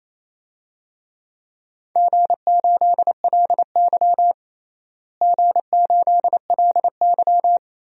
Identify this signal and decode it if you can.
Morse code, heard as a keyed tone, about 27 wpm, 700 Hz. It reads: G8LY G8LY
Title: DXCC-100 Call Signs - 4 Characters - Repeated 2x - 28wpm